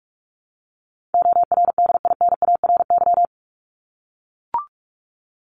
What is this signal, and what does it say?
Morse code, heard as a keyed tone, about 44 wpm, 700 Hz. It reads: ORDINARY
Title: Top 1751-2000 Words 45wpm